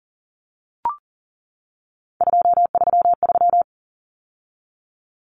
Morse code, heard as a keyed tone, about 40 wpm, 700 Hz. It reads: 233
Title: Numbers - 3-Digit 40wpm